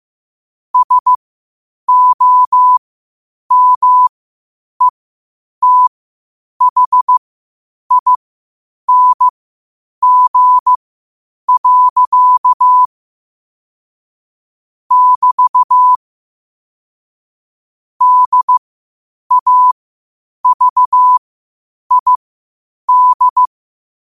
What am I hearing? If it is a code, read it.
Morse code, heard as a keyed tone, about 15 wpm, spaced as for 8 wpm, 1000 Hz. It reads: SOMETHING. <BT> DAVID
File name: qotdfiles / 8wpm